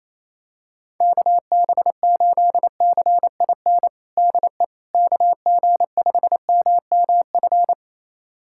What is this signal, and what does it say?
Morse code, heard as a keyed tone, about 28 wpm, 700 Hz. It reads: KB8CIN DE KG5MMF
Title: USA Callsign DE Callsign - 6 Characters - 28wpm